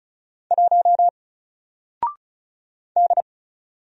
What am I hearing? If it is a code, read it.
Morse code, heard as a keyed tone, about 36 wpm, 700 Hz. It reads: 1 D
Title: Single Character (Letter:Number:Punctuation:Prosign) 35wpm